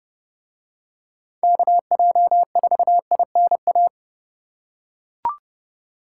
Morse code, heard as a keyed tone, about 29 wpm, 700 Hz. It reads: KJ4INA